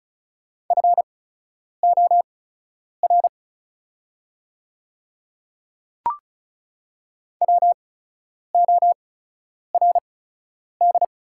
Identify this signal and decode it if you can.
Morse code, heard as a keyed tone, about 36 wpm, spaced as for 10 wpm, 700 Hz. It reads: FOR WORD
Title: 8x Character Spacing - Top 100 Words - 35wpm